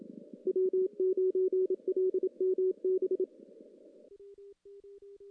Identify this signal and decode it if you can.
Morse code, heard as a keyed tone, about 28 wpm, 370 Hz. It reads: W9LMB
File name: contests / CW-SS-k5tr- -1600